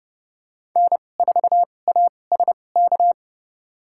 Morse code, heard as a keyed tone, about 30 wpm, spaced as for 22 wpm, 700 Hz. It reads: N4ASK
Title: USA Call Signs - 5 Characters - 2x Character Spacing - 30wpm